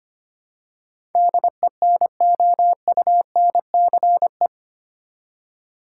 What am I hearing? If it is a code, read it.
Morse code, heard as a keyed tone, about 25 wpm, 700 Hz. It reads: DENOUNCE
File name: Eight Letter Words - 25wpm